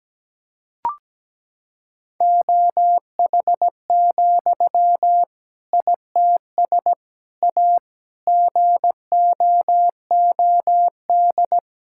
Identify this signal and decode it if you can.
Morse code, heard as a keyed tone, about 17 wpm, 700 Hz. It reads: OH, ITS A GOOD
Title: Sentences from Top 1000 Words - Review Part 5 - 17wpm